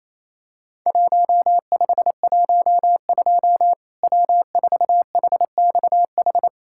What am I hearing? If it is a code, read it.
Morse code, heard as a keyed tone, about 28 wpm, 700 Hz. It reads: 1512 W4HXH
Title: Worked All Europe DX Contest - QTC Exchange - 28wpm